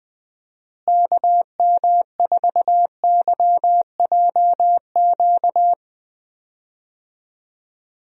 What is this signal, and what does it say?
Morse code, heard as a keyed tone, about 20 wpm, 700 Hz. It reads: KM4YJQ
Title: USA Call Signs - 6 Characters - Repeated 3x - 20wpm